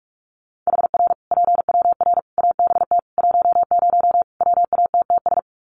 Morse code, heard as a keyed tone, about 44 wpm, 700 Hz. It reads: HR PWR ABT 10 WATTS
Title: Common QSO Phrases 45wpm